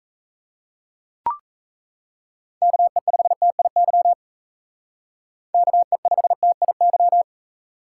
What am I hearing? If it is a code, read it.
Morse code, heard as a keyed tone, about 34 wpm, 700 Hz. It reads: KE5TIY KE5TIY
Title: Speed Racing - USA Call Signs 28wpm